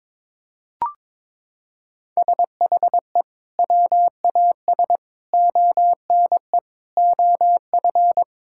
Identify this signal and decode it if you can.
Morse code, heard as a keyed tone, about 21 wpm, 700 Hz. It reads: SHE WAS ONE OF
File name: practice / 22wpm / Sentences with 4-Word Phrases - 22wpm